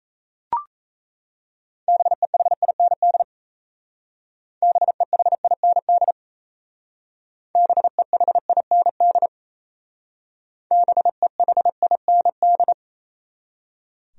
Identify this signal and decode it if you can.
Morse code, heard as a keyed tone, about 35 wpm, 700 Hz. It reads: BEHIND BEHIND BEHIND BEHIND
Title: Speed-Racing Top 500 Words 28wpm